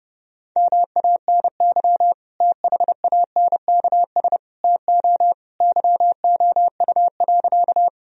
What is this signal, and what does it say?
Morse code, heard as a keyed tone, about 32 wpm, 700 Hz. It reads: MANY THANKS TO YOU.
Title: Sentences from Top 1000 Words - Review Part 3 - 30wpm